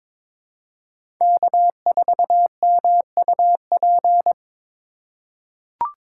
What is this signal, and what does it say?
Morse code, heard as a keyed tone, about 21 wpm, 700 Hz. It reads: K4MUP
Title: USA Call Signs - 5 Characters - 2x Character Spacing - 22wpm